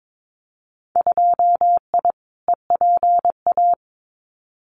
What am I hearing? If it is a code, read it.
Morse code, heard as a keyed tone, about 22 wpm, 700 Hz. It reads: 2I EPA
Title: Winter Field Day Exchanges 22wpm